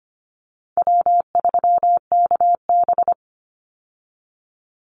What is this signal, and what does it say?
Morse code, heard as a keyed tone, about 25 wpm, 700 Hz. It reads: W3KB
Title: Easy Call Signs - 25wpm